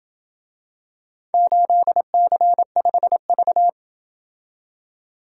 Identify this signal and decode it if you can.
Morse code, heard as a keyed tone, about 27 wpm, 700 Hz. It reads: 8C5V